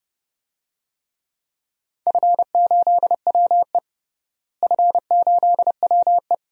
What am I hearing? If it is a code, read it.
Morse code, heard as a keyed tone, about 31 wpm, 700 Hz. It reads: F8WE F8WE
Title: DXCC-100 Call Signs - 4 Characters - Repeated 2x - 30wpm